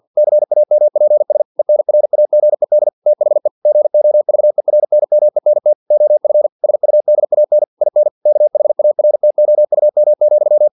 Morse code, heard as a keyed tone, about 50 wpm, 600 Hz. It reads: CAMPS ENRAGED THE GOVERNMENT OF SUDAN IN KHARTOUM,
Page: Never 49wpm